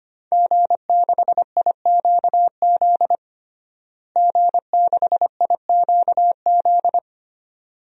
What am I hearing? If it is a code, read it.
Morse code, heard as a keyed tone, about 25 wpm, 700 Hz. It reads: G6IQZ G6IQZ
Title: DXCC-100 Call Signs - 5 Characters - Repeated 2x - 25wpm